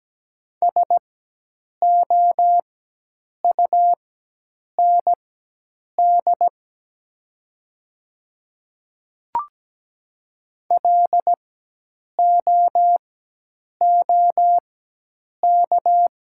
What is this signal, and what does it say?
Morse code, heard as a keyed tone, about 17 wpm, spaced as for 8 wpm, 700 Hz. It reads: SOUND LOOK
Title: 4x Character Spacing - Top 100 Words - 17wpm